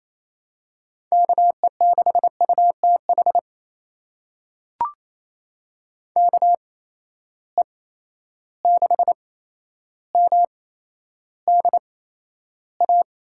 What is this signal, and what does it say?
Morse code, heard as a keyed tone, about 27 wpm, spaced as for 12 wpm, 700 Hz. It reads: KE6UTH KE6MDA
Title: USA Call Signs - 6 Characters - 8x Character Spacing - 28wpm